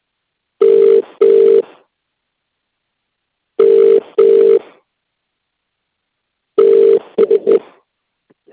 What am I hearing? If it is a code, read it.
Ringback tone (UK), 400 + 450 Hz.